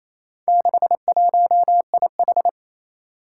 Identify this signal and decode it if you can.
Morse code, heard as a keyed tone, about 27 wpm, 700 Hz. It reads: 61IH
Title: Four Challenging Letter-Number Combinations 28wpm